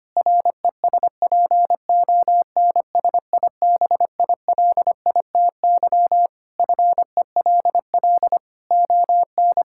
Morse code, heard as a keyed tone, about 25 wpm, 700 Hz. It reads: RESPONSIBILITY FELL ON
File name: practice / 25wpm / Sentences with 2-Word Phrases - 25wpm